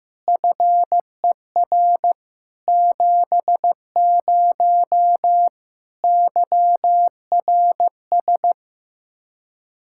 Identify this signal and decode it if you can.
Morse code, heard as a keyed tone, about 15 wpm, 700 Hz. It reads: FER 70 YRS
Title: Practice QSO - Exchange 3 - More Info - 15wpm